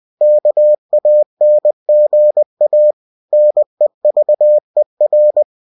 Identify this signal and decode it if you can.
Morse code, heard as a keyed tone, about 20 wpm, 600 Hz. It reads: KANGA NEVER
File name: cw-books / winnie-the-pooh / winnie-20wpm-page-0063